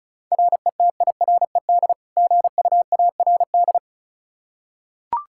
Morse code, heard as a keyed tone, about 36 wpm, 700 Hz. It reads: RETIRED GUARD
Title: K1USN SST QSO Practice 35wpm